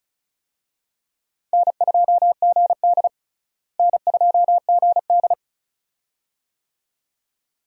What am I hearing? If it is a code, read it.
Morse code, heard as a keyed tone, about 34 wpm, 700 Hz. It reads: N2GD N2GD
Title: USA Call Signs - 4 Characters - Repeated 2x - 35wpm